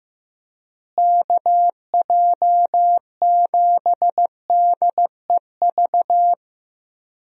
Morse code, heard as a keyed tone, about 15 wpm, 700 Hz. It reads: KJ7DEV
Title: USA Call Signs - 6 Characters - Repeated 2x - 15wpm